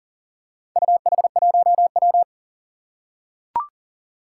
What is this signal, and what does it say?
Morse code, heard as a keyed tone, about 37 wpm, 700 Hz. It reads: UH1W
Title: Four Challenging Letter-Number Combinations 40wpm